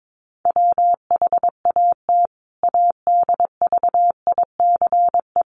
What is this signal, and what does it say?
Morse code, heard as a keyed tone, about 22 wpm, 700 Hz. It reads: WHAT ADVICE
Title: Sentences from Top 1100 Words 22wpm